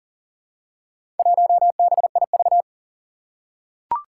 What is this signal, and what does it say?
Morse code, heard as a keyed tone, about 37 wpm, 700 Hz. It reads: 1BIV
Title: Four Challenging Letter-Number Combinations 40wpm